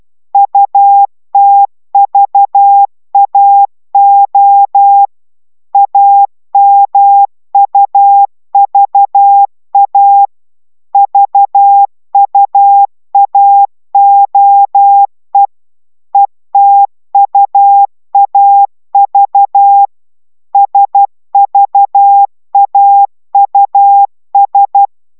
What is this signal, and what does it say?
Morse code, heard as a keyed tone, about 12 wpm, 800 Hz. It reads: UTVAO AMUVA VUAOE ETUAV SVAUS